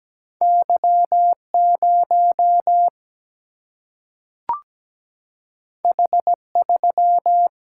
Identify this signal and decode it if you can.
Morse code, heard as a keyed tone, about 17 wpm, 700 Hz. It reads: Y 0 H 3
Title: Single-Letter Single-Number 17wpm